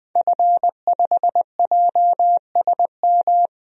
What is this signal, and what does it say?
Morse code, heard as a keyed tone, about 20 wpm, 700 Hz. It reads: F5JSM
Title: DXCC-100 Call Signs - 5 Characters - Repeated 3x - 20wpm